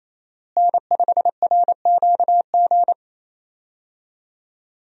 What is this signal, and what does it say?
Morse code, heard as a keyed tone, about 28 wpm, 700 Hz. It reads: N5RQG
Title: USA Call Signs - 5 Characters - Repeated 3x - 28wpm